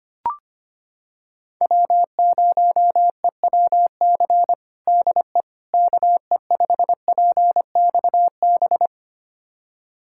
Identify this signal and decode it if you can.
Morse code, heard as a keyed tone, about 25 wpm, 700 Hz. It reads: W0EWC DE KE5PXB
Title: USA Callsign DE Callsign 25wpm